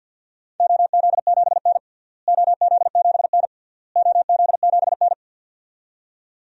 Morse code, heard as a keyed tone, about 50 wpm, 700 Hz. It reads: OZ7N OZ7N OZ7N
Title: DXCC-100 Call Signs - 4 Characters - Repeated 3x - 50wpm